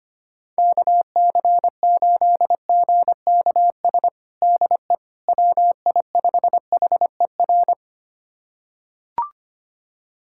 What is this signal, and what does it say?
Morse code, heard as a keyed tone, about 25 wpm, 700 Hz. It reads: KC8GKS DE WI5HER
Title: USA Callsign DE Callsign 25wpm